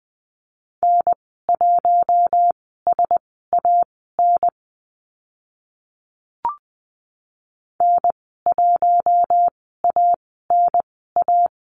Morse code, heard as a keyed tone, about 20 wpm, spaced as for 14 wpm, 700 Hz. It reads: N1SAN N1ANA
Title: 042 Lesson 8g - Number 1 - Call Signs 20wpm